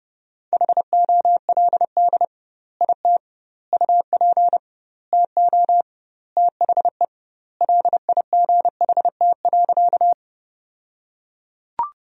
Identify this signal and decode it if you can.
Morse code, heard as a keyed tone, about 32 wpm, 700 Hz. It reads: HOLD IT UP TO THE LIGHT.
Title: 2x Word-Spacing - Sentences from Top 400 Words - 30wpm